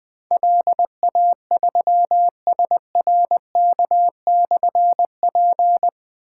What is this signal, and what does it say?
Morse code, heard as a keyed tone, about 21 wpm, 700 Hz. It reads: LA3SRK/P
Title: Difficult Call Signs 20wpm